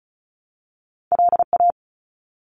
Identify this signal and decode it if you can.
Morse code, heard as a keyed tone, about 34 wpm, 700 Hz. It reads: LA
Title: ARRL Sections 35wpm